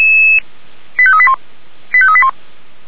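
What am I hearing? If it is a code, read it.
Five-tone selective call (ZVEI), heard as quick runs of five tones, tones of about 70 ms. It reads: 85381 85381
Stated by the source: ZVEI five-tone selective call received on an SDR.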